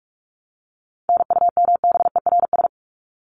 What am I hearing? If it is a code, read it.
Morse code, heard as a keyed tone, about 44 wpm, 700 Hz. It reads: NUMBERS
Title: ES:S Suffix - Top 100 Words with Base Words 45wpm